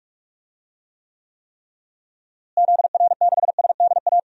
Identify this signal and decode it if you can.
Morse code, heard as a keyed tone, about 46 wpm, 700 Hz. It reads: ZR6SDA
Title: DXCC-100 Call Signs - 6 Characters - Repeated 3x - 45wpm